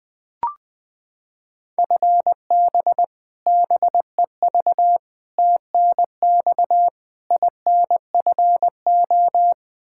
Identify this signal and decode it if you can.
Morse code, heard as a keyed tone, about 20 wpm, 700 Hz. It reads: FB BEV TNX INFO
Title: Practice QSO - Exchange 3 - More Info - 20wpm